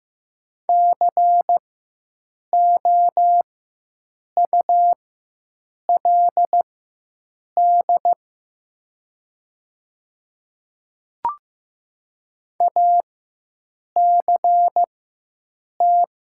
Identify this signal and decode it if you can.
Morse code, heard as a keyed tone, about 15 wpm, spaced as for 7 wpm, 700 Hz. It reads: COULD ACT
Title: 4x Character Spacing - Top 300 Words - 15wpm